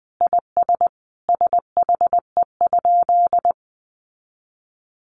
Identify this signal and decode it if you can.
Morse code, heard as a keyed tone, about 20 wpm, 700 Hz. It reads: IS SHE?
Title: Sentences from Top 100 Words 20wpm